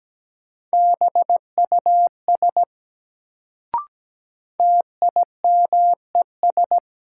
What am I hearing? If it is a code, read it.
Morse code, heard as a keyed tone, about 17 wpm, 700 Hz. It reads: BUS TIMES
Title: Sentences from Top 1000 Words - Review Part 6 - New Word Drill - 17wpm